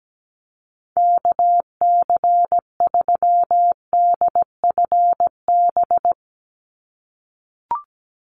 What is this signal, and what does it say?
Morse code, heard as a keyed tone, about 17 wpm, 700 Hz. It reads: KC3DFB
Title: USA Call Signs 17wpm